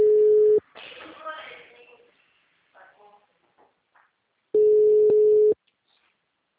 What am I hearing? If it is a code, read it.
Ringback tone (Europe, 425 Hz), with speech.